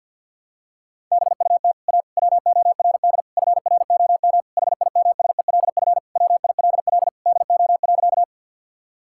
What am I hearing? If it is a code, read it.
Morse code, heard as a keyed tone, about 51 wpm, 700 Hz. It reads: BUT A WORD FROM HIMSELF WILL DO.
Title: Sentences from Top 1600 Words 50wpm